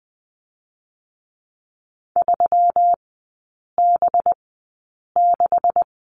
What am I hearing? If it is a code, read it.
Morse code, heard as a keyed tone, about 20 wpm, 700 Hz. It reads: 3 B 6
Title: 184 Lesson 30e - Letter b - Three Characters 20wpm